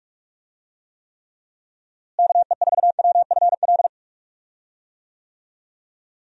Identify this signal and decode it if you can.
Morse code, heard as a keyed tone, about 46 wpm, 700 Hz. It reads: KE4WFL
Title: USA Call Signs - 6 Characters - Repeated 3x - 45wpm